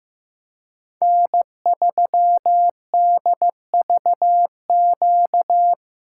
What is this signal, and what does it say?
Morse code, heard as a keyed tone, about 15 wpm, 700 Hz. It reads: N3DVQ
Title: USA Call Signs - 3x Character Spacing - 15wpm